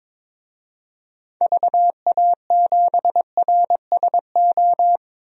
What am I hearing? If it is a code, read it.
Morse code, heard as a keyed tone, about 21 wpm, 700 Hz. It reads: VA7RSO
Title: DXCC-100 Call Signs - 6 Characters - 22wpm